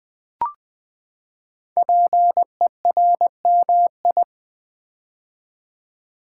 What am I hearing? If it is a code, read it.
Morse code, heard as a keyed tone, about 20 wpm, 700 Hz. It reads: PERMI
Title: Build A Word - 5 Letters At A Time - 10-Letter Words - 20wpm